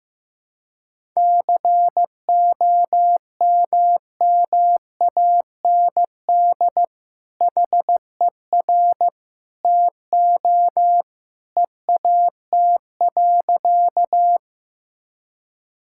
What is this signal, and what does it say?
Morse code, heard as a keyed tone, about 15 wpm, 700 Hz. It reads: COMMAND HER TO EAT.
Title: Sentences from Top 1600 Words 15wpm